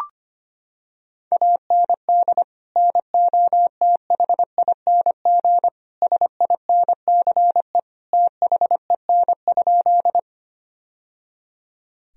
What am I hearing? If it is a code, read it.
Morse code, heard as a keyed tone, about 25 wpm, 700 Hz. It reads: AND NOTHING SINCE THEN?
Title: Sentences from Top 1000 Words - Review Part 1 - 25wpm